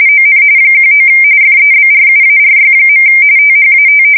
RTTY, about 45.45 baud. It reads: ON 19 FEBRUARY. THAT
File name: fsk170 / USS-Missouri